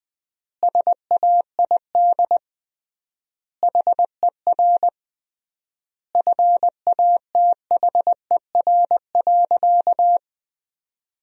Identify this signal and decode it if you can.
Morse code, heard as a keyed tone, about 20 wpm, 700 Hz. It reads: SAID HER FATHER.
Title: 3x Word-Spacing - Sentences from Top 300 Words - 20wpm